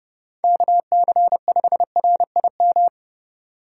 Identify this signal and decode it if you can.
Morse code, heard as a keyed tone, about 30 wpm, 700 Hz. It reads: KC5RIM
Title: USA Call Signs - 4x Character Spacing - 30wpm